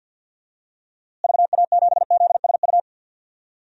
Morse code, heard as a keyed tone, about 52 wpm, 700 Hz. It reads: VA7ZSU
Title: DXCC-100 Call Signs - 6 Characters - Repeated 2x - 50wpm